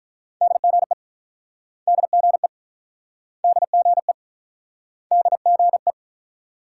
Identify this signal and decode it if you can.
Morse code, heard as a keyed tone, about 47 wpm, 700 Hz. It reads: DGE DGE DGE DGE